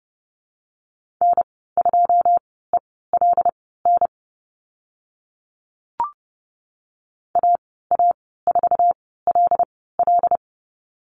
Morse code, heard as a keyed tone, about 30 wpm, spaced as for 17 wpm, 700 Hz. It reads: N2ELN AA4LL
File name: Lesson 15g - Number 5 - Call Signs 30wpm